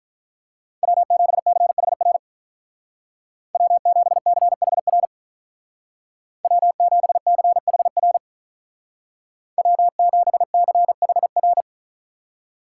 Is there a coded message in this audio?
Yes: Morse code from 0.8 to 11.6 s, heard as a keyed tone, about 47 wpm, 700 Hz. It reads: W7CHR W7CHR W7CHR W7CHR